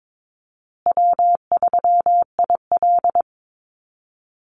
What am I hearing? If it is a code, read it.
Morse code, heard as a keyed tone, about 22 wpm, 700 Hz. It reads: W3IL
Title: Easy Call Signs Repeated 3x - 22wpm